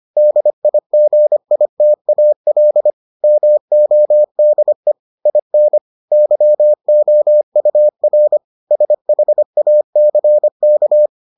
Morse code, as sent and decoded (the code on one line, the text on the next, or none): -.. .. --. .. - .- .-.. / -- --- -.. / .. -. / -.-- --- ..- .-. / ... .... .- -.-. -.-
DIGITAL MODE IN YOUR SHACK